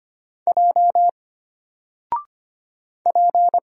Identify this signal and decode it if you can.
Morse code, heard as a keyed tone, about 25 wpm, 700 Hz. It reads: J P
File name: Single-Letter-Number-25wpm